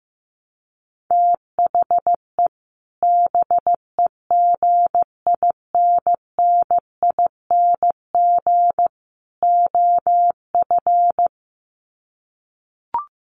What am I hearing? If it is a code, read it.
Morse code, heard as a keyed tone, about 15 wpm, 700 Hz. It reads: THE BEGINNING OF
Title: Sets of 3 Words 15wpm